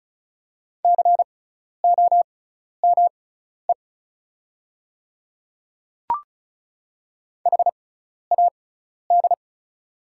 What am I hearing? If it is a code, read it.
Morse code, heard as a keyed tone, about 34 wpm, spaced as for 12 wpm, 690 Hz. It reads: COME HAD
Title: 6x Character Spacing - Top 500 Words - 35wpm